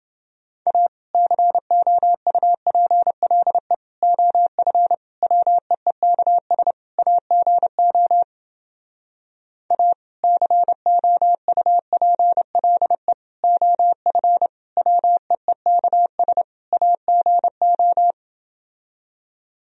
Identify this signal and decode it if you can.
Morse code, heard as a keyed tone, about 30 wpm, 700 Hz. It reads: A COUPLE OF WEEKS AGO A COUPLE OF WEEKS AGO